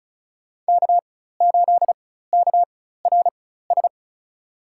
Morse code, heard as a keyed tone, about 34 wpm, spaced as for 16 wpm, 700 Hz. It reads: K8KRS
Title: USA Call Signs - 5 Characters - 4x Character Spacing - 35wpm